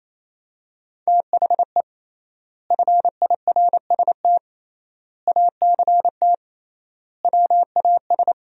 Morse code, heard as a keyed tone, about 28 wpm, 700 Hz. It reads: THE FIRST ACT WAS
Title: 3x Word-Spacing - Sentences from Top 200 Words - 28wpm